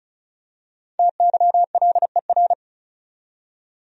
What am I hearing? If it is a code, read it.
Morse code, heard as a keyed tone, about 36 wpm, 700 Hz. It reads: TYLER
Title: USA Common Names - Max 5 Letters - 35wpm